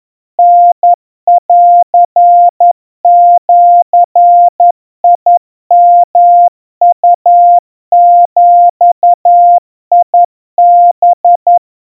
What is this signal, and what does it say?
Morse code (Japanese Wabun code), heard as a keyed tone, about 11 wpm, 700 Hz. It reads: タンジヨウビハ
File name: kawa / Morse sample / kamyu 40 cpm 700 Hz Crispy